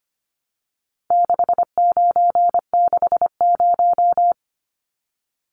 Morse code, heard as a keyed tone, about 25 wpm, 700 Hz. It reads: 6960
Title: Numbers - 4-Digit 25wpm